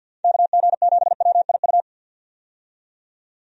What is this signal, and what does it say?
Morse code, heard as a keyed tone, about 50 wpm, 700 Hz. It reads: KG7WIU
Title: USA Call Signs - 6 Characters - 50wpm